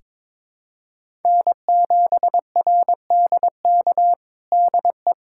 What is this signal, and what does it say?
Morse code, heard as a keyed tone, about 22 wpm, 700 Hz. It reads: N7RDK DE
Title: USA Callsign DE Callsign - 5 Characters - 22wpm